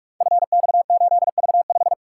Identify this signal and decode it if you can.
Morse code, heard as a keyed tone, about 44 wpm, 700 Hz. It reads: FX8V5
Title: Code Groups - 5 Letter:Numbers - 45wpm